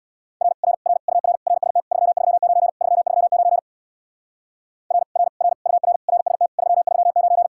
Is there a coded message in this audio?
Yes: Morse code from 0.4 to 7.5 s, heard as a keyed tone, about 66 wpm, 700 Hz. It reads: R R R UR RST 339 339 R R R UR RST 339